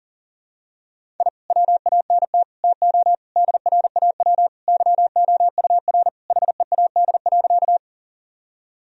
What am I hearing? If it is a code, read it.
Morse code, heard as a keyed tone, about 37 wpm, 700 Hz. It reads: I WANT TO DRAW YOUR HEAD.
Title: Sentences from Top 1000 Words - Review Part 1 - 40wpm